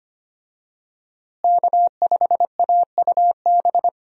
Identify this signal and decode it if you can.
Morse code, heard as a keyed tone, about 25 wpm, 700 Hz. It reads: K5AUB
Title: USA Call Signs - 5 Characters - Repeated 2x - 25wpm